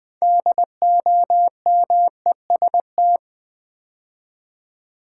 Morse code, heard as a keyed tone, about 20 wpm, 700 Hz. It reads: DOMEST
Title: Build A Word - 3 Letters At A Time - 8-Letter Words - 20wpm